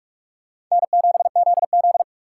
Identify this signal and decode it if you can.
Morse code, heard as a keyed tone, about 47 wpm, 700 Hz. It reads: N7ZZ
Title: USA Call Signs - 4 Characters - 2x Character Spacing - 45wpm